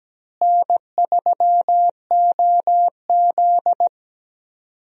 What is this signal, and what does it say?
Morse code, heard as a keyed tone, about 17 wpm, 700 Hz. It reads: N3OZ
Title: USA Call Signs - 4 Characters - Repeated 2x - 17wpm